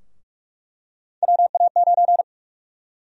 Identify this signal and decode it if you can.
Morse code, heard as a keyed tone, about 44 wpm, 700 Hz. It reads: WA9
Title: USA Call Sign Prefixes - 45wpm